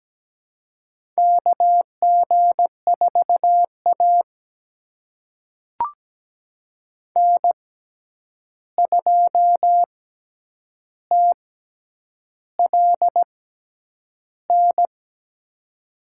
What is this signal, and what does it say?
Morse code, heard as a keyed tone, about 17 wpm, spaced as for 8 wpm, 700 Hz. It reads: KG4A N2TLN
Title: USA Call Signs - 6x Character Spacing - 17wpm